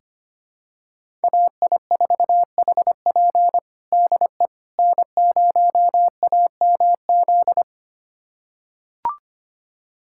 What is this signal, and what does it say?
Morse code, heard as a keyed tone, about 25 wpm, 700 Hz. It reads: AI4HP DE N0AMZ
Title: USA Callsign DE Callsign - 5 Characters - 25wpm